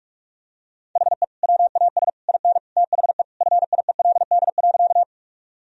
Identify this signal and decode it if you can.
Morse code, heard as a keyed tone, about 41 wpm, 700 Hz. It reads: HE WAS IN THE FIELD.
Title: Sentences from Top 1000 Words - Review Part 2 - 45wpm